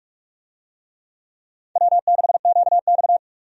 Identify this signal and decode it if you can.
Morse code, heard as a keyed tone, about 46 wpm, 700 Hz. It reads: W6QX
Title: USA Call Signs - 4 Characters - 45wpm